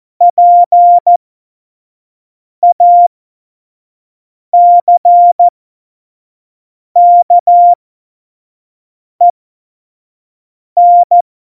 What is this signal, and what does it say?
Morse code, heard as a keyed tone, about 14 wpm, spaced as for 5 wpm, 700 Hz. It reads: PACKEN
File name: Morsekurs / Witze / Witz 22. Januar 2019, 25 F70